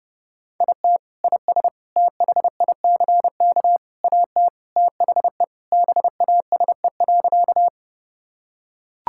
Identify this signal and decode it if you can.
Morse code, heard as a keyed tone, about 32 wpm, 700 Hz. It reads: IT IS THICK AT THE BASE.
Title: Sentences from Top 1600 Words 30wpm